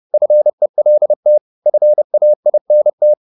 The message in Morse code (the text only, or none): FELT FAINT